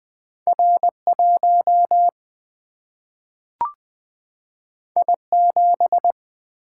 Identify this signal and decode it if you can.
Morse code, heard as a keyed tone, about 20 wpm, 700 Hz. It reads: R1 I7
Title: Single-Letter Single-Number 20wpm